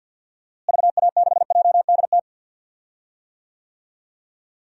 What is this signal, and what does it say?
Morse code, heard as a keyed tone, about 47 wpm, 700 Hz. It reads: VA6JDT
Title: DXCC-100 Call Signs - 6 Characters - Repeated 3x - 50wpm